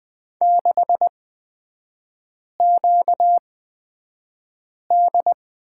Morse code, heard as a keyed tone, about 20 wpm, spaced as for 5 wpm, 700 Hz. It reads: 6QD
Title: 250 Lesson 40e - BK - Three Characters 20wpm